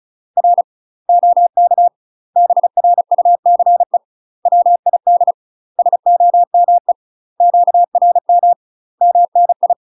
Morse code, heard as a keyed tone, about 36 wpm, 700 Hz. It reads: R OK BRUCE WID SOME QRM MNI
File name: UK MASTER QSOs TRAINING / 35WPM / UK MASTER QSOs 0009